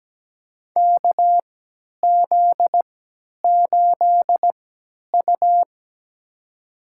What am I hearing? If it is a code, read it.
Morse code, heard as a keyed tone, about 17 wpm, spaced as for 10 wpm, 700 Hz. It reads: KZ8U
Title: USA Call Signs - 4 Characters - 3x Character Spacing - 17wpm